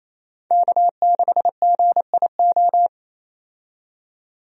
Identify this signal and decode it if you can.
Morse code, heard as a keyed tone, about 27 wpm, 700 Hz. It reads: K6GIO
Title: USA Call Signs - 5 Characters - 28wpm